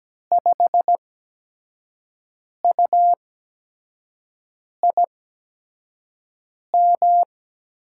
Morse code, heard as a keyed tone, about 17 wpm, spaced as for 5 wpm, 700 Hz. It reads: 5UIM
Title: USA Call Signs - 6 Characters - 8x Character Spacing - 17wpm